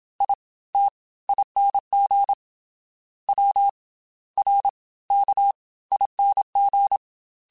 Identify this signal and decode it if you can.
Morse code, heard as a keyed tone, about 27 wpm, spaced as for 18 wpm, 800 Hz. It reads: ITING WRKING